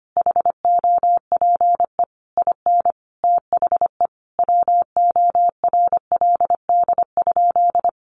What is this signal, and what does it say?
Morse code, heard as a keyed tone, about 25 wpm, 700 Hz. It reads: HOPE IN THE WORLD?
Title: Sentences from Top 1400 Words 25wpm